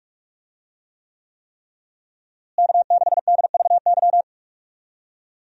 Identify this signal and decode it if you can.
Morse code, heard as a keyed tone, about 45 wpm, 700 Hz. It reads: K6DVY
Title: USA Call Signs - 5 Characters - Repeated 3x - 45wpm